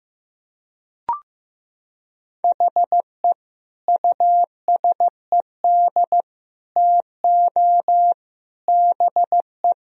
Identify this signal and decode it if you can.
Morse code, heard as a keyed tone, about 15 wpm, 700 Hz. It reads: HE USED TO BE
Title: Sentences from Top 1000 Words - Review Part 4 - 15wpm